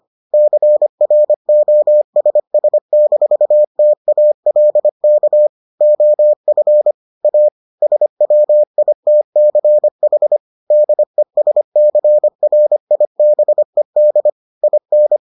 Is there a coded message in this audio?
Yes: Morse code from 0.3 to 15.2 s, heard as a keyed tone, about 25 wpm, 600 Hz. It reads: CROSS-TALK OF A SWITCH DESCRIBED IN